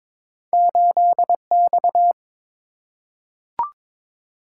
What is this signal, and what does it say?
Morse code, heard as a keyed tone, about 22 wpm, 700 Hz. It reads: 8X
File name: Two Letter-Number Combinations - 22wpm